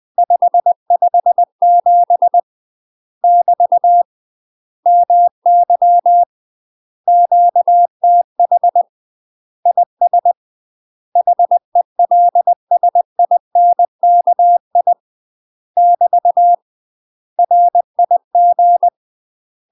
Morse code, heard as a keyed tone, about 20 wpm, 700 Hz. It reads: 557 <BT> MY QTH IS HELSINKI <BT> RIG
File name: UK MASTER QSOs TRAINING / 20WPM / UK MASTER QSOs 0048